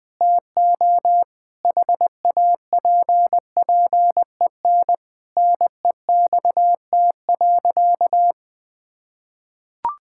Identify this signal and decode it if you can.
Morse code, heard as a keyed tone, about 20 wpm, 700 Hz. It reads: TO HAPPEN NEXT.
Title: Sentences with 7-Word Phrases - Spoken Beginning - 20wpm